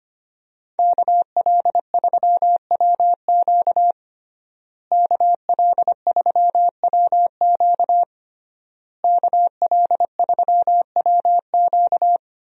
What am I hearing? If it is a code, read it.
Morse code, heard as a keyed tone, about 25 wpm, 700 Hz. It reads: KL3WQ KL3WQ KL3WQ
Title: USA Call Signs - 5 Characters - Repeated 3x - 25wpm